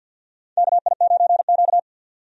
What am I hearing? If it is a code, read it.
Morse code, heard as a keyed tone, about 47 wpm, 700 Hz. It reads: KI9Q
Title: USA Call Signs - 4 Characters - 50wpm